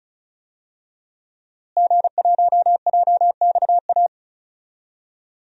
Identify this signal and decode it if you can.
Morse code, heard as a keyed tone, about 34 wpm, 700 Hz. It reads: G1JXA
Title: DXCC-100 Call Signs - 5 Characters - Repeated 2x - 35wpm